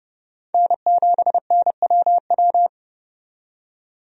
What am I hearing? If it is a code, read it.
Morse code, heard as a keyed tone, about 30 wpm, 700 Hz. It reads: N7NWW
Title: USA Call Signs - 5 Characters - 30wpm